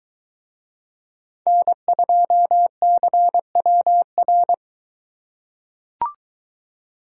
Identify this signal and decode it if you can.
Morse code, heard as a keyed tone, about 21 wpm, 700 Hz. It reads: N2CWR